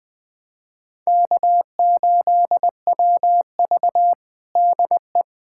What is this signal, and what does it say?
Morse code, heard as a keyed tone, about 20 wpm, 700 Hz. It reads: K8WV DE
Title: USA Callsign DE Callsign - 4 Characters - 20wpm